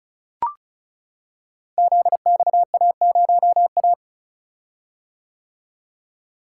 Morse code, heard as a keyed tone, about 34 wpm, 700 Hz. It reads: ZXA0A
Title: Code Groups - 5 Letter:Numbers - 35wpm